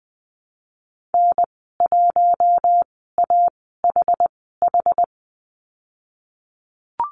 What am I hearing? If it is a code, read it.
Morse code, heard as a keyed tone, about 20 wpm, spaced as for 15 wpm, 700 Hz. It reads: N1AHH
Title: 161 Lesson 26g - Letter g - Call Signs 20wpm